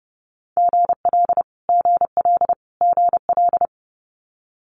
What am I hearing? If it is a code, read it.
Morse code, heard as a keyed tone, about 30 wpm, 700 Hz. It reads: GL GL GL
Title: Multi-letter Phonemes Repeated 3x 30wpm